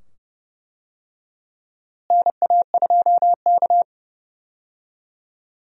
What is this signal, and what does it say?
Morse code, heard as a keyed tone, about 32 wpm, 700 Hz. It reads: NA2K
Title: USA Call Signs - 4 Characters - Repeated 2x - 30wpm